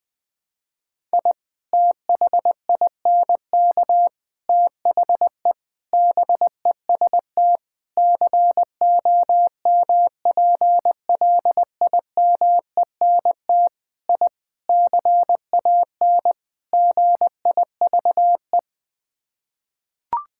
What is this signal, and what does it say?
Morse code, heard as a keyed tone, about 20 wpm, 700 Hz. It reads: I THINK THE BEST COMPLIMENT I CAN GIVE
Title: Sets of 8 Words - 20wpm